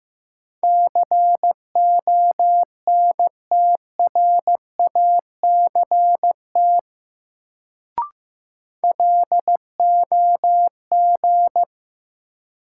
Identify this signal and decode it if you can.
Morse code, heard as a keyed tone, about 15 wpm, 700 Hz. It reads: CONTRACT LOG
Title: Sentences from Top 1400 Words - New Word Drill - 15wpm